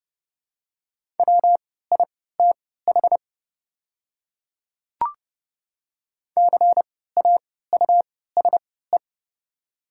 Morse code, heard as a keyed tone, about 30 wpm, spaced as for 17 wpm, 700 Hz. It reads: WITH CAUSE
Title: 3x Character Spacing - Top 200 Words - 30wpm